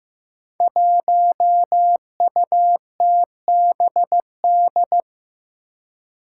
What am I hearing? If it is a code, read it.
Morse code, heard as a keyed tone, about 15 wpm, 700 Hz. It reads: 1UTBD
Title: Five Challenging Letter-Number Combinations 15wpm